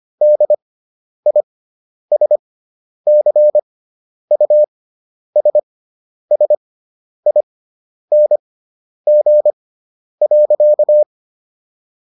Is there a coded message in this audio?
Yes: Morse code from 0.2 to 11.0 s, heard as a keyed tone, about 25 wpm, spaced as for 10 wpm, 600 Hz. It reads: DISCUSSING.